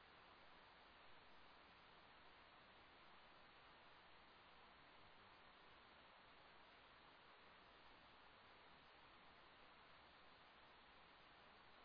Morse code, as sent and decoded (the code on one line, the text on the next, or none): none